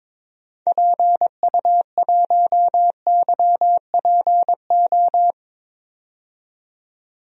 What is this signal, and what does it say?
Morse code, heard as a keyed tone, about 21 wpm, 700 Hz. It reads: PU1YPO
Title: DXCC-100 Call Signs - 6 Characters - 22wpm